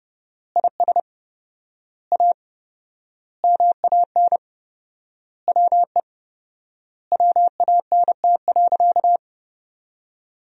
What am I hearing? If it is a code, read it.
Morse code, heard as a keyed tone, about 30 wpm, 700 Hz. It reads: IS A MAN WE WANT.
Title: 4x Word-Spacing - Sentences from Top 200 Words - 30wpm